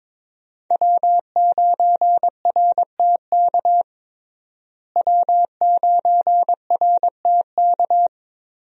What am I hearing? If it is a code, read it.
Morse code, heard as a keyed tone, about 22 wpm, 700 Hz. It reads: W9RTK W9RTK
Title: USA Call Signs - 5 Characters - Repeated 2x - 22wpm